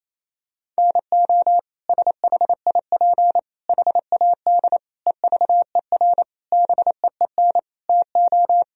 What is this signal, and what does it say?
Morse code, heard as a keyed tone, about 28 wpm, 700 Hz. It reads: NO SHIP HAD EVER BEEN TO
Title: Sentences from Top 1000 Words - Review Part 4 - 28wpm